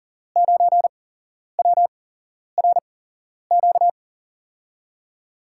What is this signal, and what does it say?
Morse code, heard as a keyed tone, about 40 wpm, spaced as for 11 wpm, 700 Hz. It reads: 9WRQ
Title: USA Call Signs - 8x Character Spacing - 40wpm